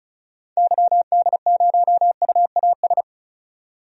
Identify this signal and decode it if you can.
Morse code, heard as a keyed tone, about 34 wpm, 700 Hz. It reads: YD0UAS
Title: DXCC-100 Call Signs - 6 Characters - 35wpm